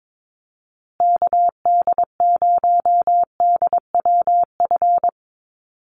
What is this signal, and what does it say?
Morse code, heard as a keyed tone, about 22 wpm, 700 Hz. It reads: KD0DWF
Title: USA Call Signs Repeated 2x 22wpm